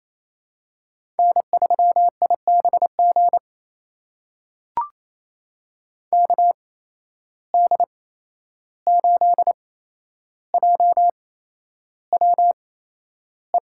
Morse code, heard as a keyed tone, about 28 wpm, spaced as for 11 wpm, 700 Hz. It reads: N3IBG KD8JWE
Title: USA Call Signs - 8x Character Spacing - 28wpm